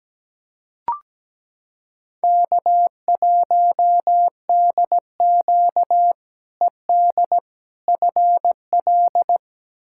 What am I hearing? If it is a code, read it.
Morse code, heard as a keyed tone, about 17 wpm, 700 Hz. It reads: K1DQ ED FL
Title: K1USN SST QSO Practice - 17wpm